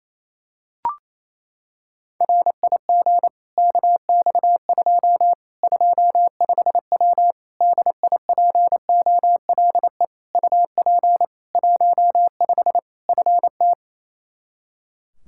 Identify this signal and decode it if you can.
Morse code, heard as a keyed tone, about 28 wpm, 700 Hz. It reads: RIG KX2 25W DIPOLE UP 15 FT
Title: Practice QSO - Exchange 2 - Info - 28wpm